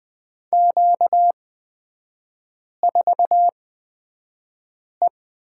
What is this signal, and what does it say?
Morse code, heard as a keyed tone, about 20 wpm, spaced as for 5 wpm, 690 Hz. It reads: Q4E
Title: 255 Lesson 41e - Period - Three Characters 20wpm